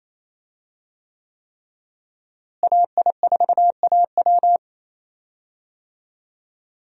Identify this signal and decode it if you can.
Morse code, heard as a keyed tone, about 28 wpm, 700 Hz. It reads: AI4AW
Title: USA Call Signs - 5 Characters - Repeated 3x - 28wpm